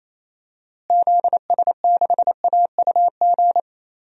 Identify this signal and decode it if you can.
Morse code, heard as a keyed tone, about 28 wpm, 700 Hz. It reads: ZS6AUG
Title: DXCC-100 Call Signs - 6 Characters - Repeated 2x - 28wpm